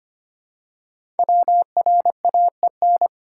Morse code, heard as a keyed tone, about 25 wpm, 700 Hz. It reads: WRAEN
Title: Code Groups - 5 Letters - 25wpm